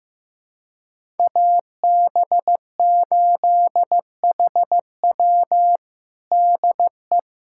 Morse code, heard as a keyed tone, about 15 wpm, 700 Hz. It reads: AB8HW DE
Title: USA Callsign DE Callsign - 5 Characters - 15wpm